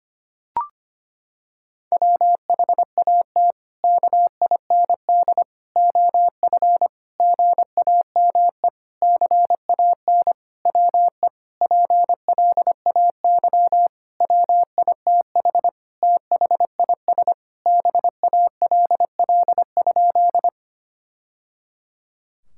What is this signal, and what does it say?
Morse code, heard as a keyed tone, about 25 wpm, 700 Hz. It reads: WHAT KIND OF GAME CAN WE PLAY WITH THIS BALL?
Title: Sentences from Top 1000 Words - Review Part 6 - 25wpm